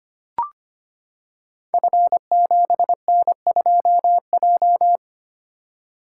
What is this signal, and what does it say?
Morse code, heard as a keyed tone, about 25 wpm, 700 Hz. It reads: F7N2J
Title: Code Groups - 5 Letter:Numbers - 25wpm